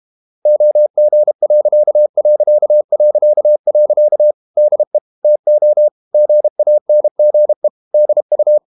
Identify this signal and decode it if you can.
Morse code, heard as a keyed tone, about 32 wpm, 600 Hz. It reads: OG.... DE TO GANGE DU
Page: Ferie 32 wpm